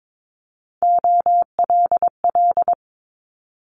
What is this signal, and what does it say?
Morse code, heard as a keyed tone, about 22 wpm, 700 Hz. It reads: OLL
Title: Multi-letter Phonemes Quick Fire 22wpm